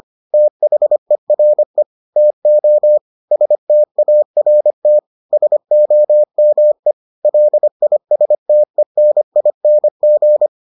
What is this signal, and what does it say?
Morse code, heard as a keyed tone, about 25 wpm, 600 Hz. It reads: THERE TO START SOME LISTENING